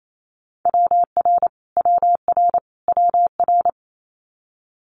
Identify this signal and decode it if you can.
Morse code, heard as a keyed tone, about 28 wpm, 700 Hz. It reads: WR WR WR
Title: Multi-letter Phonemes Repeated 3x 28wpm